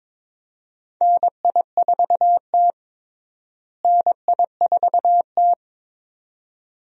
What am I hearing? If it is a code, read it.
Morse code, heard as a keyed tone, about 21 wpm, 700 Hz. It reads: NI4T NI4T
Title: USA Call Signs - 4 Characters - Repeated 2x - 22wpm